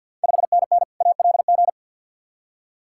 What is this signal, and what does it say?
Morse code, heard as a keyed tone, about 50 wpm, 690 Hz. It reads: 5NN ALG
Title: Ontario QSO Party Exchange 50wpm